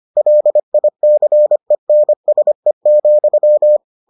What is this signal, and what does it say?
Morse code, heard as a keyed tone, about 25 wpm, 600 Hz. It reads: LICENSE,